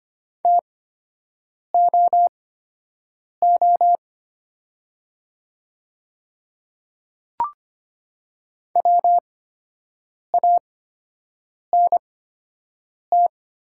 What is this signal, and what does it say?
Morse code, heard as a keyed tone, about 25 wpm, spaced as for 7 wpm, 700 Hz. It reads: TOO WANT
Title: 8x Character Spacing - Top 300 Words - 25wpm